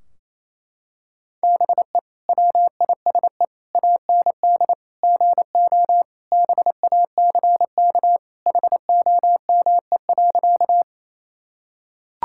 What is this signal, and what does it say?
Morse code, heard as a keyed tone, about 28 wpm, 700 Hz. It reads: BE WISE AND GO BACK HOME.
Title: Sentences from Top 1600 Words 28wpm